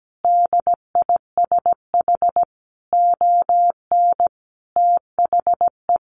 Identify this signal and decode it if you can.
Morse code, heard as a keyed tone, about 17 wpm, 700 Hz. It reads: DISH ON THE
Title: Sentences from Top 1300 Words 17wpm